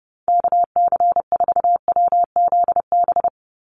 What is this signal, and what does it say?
Morse code, heard as a keyed tone, about 30 wpm, 700 Hz. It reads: KC4WZB
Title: NA SOTA Activator Callsigns 30wpm